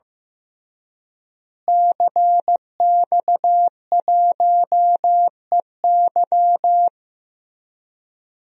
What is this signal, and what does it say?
Morse code, heard as a keyed tone, about 16 wpm, 700 Hz. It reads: CX1EY
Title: DXCC-100 Call Signs - 5 Characters - Repeated 3x - 15wpm